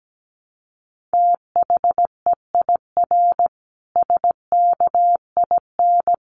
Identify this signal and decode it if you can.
Morse code, heard as a keyed tone, about 17 wpm, 700 Hz. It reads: THEIR SKIN
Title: Sentences from Top 1300 Words 17wpm